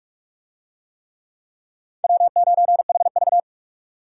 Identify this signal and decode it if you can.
Morse code, heard as a keyed tone, about 44 wpm, 700 Hz. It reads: W9HV